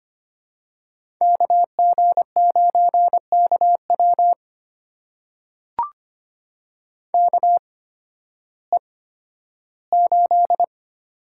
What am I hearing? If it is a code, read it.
Morse code, heard as a keyed tone, about 25 wpm, spaced as for 13 wpm, 700 Hz. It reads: KG9KW KE8
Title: USA Call Signs - 8x Character Spacing - 25wpm